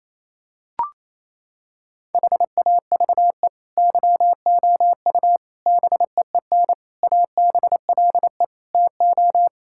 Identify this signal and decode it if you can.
Morse code, heard as a keyed tone, about 28 wpm, 700 Hz. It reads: HAVE YOU BEEN ABLE TO
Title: Sentences from Top 1000 Words - Review Part 6 - 28wpm